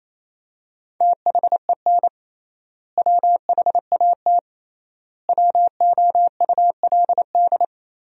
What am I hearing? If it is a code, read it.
Morse code, heard as a keyed tone, about 28 wpm, 700 Hz. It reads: THEN WHAT WOULD
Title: 3x Word-Spacing - Sentences from Top 400 Words - 28wpm